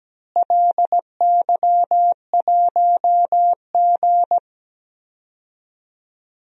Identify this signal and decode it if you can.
Morse code, heard as a keyed tone, about 17 wpm, 700 Hz. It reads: LY1G
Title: DXCC-100 Call Signs - 4 Characters - Repeated 3x - 17wpm